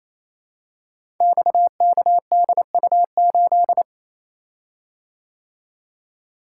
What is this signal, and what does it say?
Morse code, heard as a keyed tone, about 28 wpm, 700 Hz. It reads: XKDU8
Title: Code Groups - 5 Letter:Numbers - 28wpm